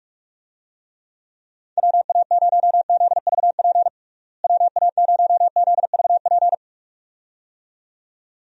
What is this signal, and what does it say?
Morse code, heard as a keyed tone, about 46 wpm, 700 Hz. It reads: WA0ZVP WA0ZVP
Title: USA Call Signs - 6 Characters - Repeated 2x - 45wpm